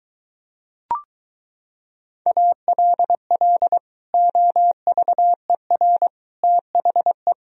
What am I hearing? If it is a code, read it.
Morse code, heard as a keyed tone, about 23 wpm, 700 Hz. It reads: ALL OVER THE